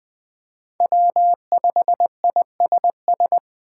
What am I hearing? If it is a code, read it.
Morse code, heard as a keyed tone, about 20 wpm, 700 Hz. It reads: W5ISS
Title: USA Call Signs - 5 Characters - 6x Character Spacing - 20wpm